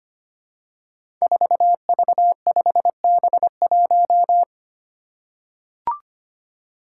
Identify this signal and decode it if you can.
Morse code, heard as a keyed tone, about 25 wpm, 700 Hz. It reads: 4V5B1
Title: Five Challenging Letter-Number Combinations 25wpm